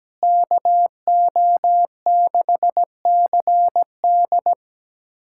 Morse code, heard as a keyed tone, about 17 wpm, 700 Hz. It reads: KO6CD
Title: USA Callsign DE Callsign 17wpm